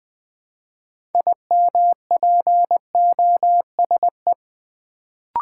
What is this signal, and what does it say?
Morse code, heard as a keyed tone, about 20 wpm, 700 Hz. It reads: IMPOSE
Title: Build A Word - 3 Letters At A Time - 6-Letter Words - 20wpm